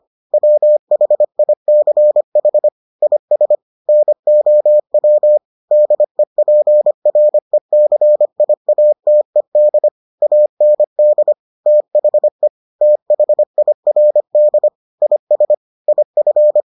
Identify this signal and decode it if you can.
Morse code, heard as a keyed tone, about 25 wpm, 600 Hz. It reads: WHICH IS NOW DEPRECIATED AND THE THIRD IS IF